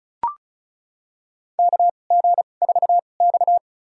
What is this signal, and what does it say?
Morse code, heard as a keyed tone, about 36 wpm, spaced as for 25 wpm, 700 Hz. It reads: KG4X
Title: USA Call Signs - 4 Characters - 2x Character Spacing - 35wpm